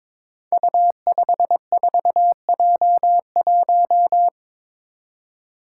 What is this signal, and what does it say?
Morse code, heard as a keyed tone, about 22 wpm, 700 Hz. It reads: U54J1
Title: Five Challenging Letter-Number Combinations 22wpm